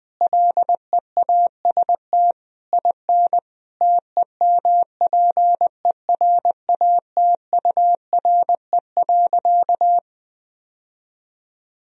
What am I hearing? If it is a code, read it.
Morse code, heard as a keyed tone, about 20 wpm, 700 Hz. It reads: LEAST IN TEMPERATURE.
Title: Sentences from Top 1000 Words - Review Part 4 - 20wpm